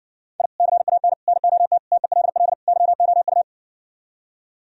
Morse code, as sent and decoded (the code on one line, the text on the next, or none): .. / -.-. .- -. / -. --- - / - . .-.. .-.. / -.-- --- ..-
I CAN NOT TELL YOU